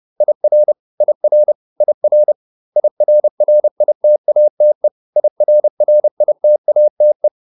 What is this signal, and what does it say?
Morse code, heard as a keyed tone, about 30 wpm, 600 Hz. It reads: IR IR IR IRRITATE IRRITATE